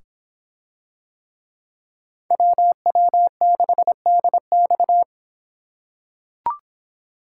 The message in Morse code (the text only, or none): WW6DX